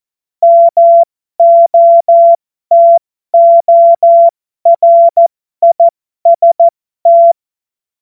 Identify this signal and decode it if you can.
Morse code, heard as a keyed tone, about 14 wpm, spaced as for 12 wpm, 680 Hz. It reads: MOTORIST
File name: Morsekurs / Witze